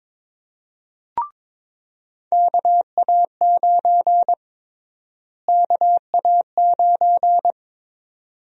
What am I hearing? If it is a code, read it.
Morse code, heard as a keyed tone, about 22 wpm, 700 Hz. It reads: KA9 KA9
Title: USA Call Sign Prefixes - Repeated 2x - 22wpm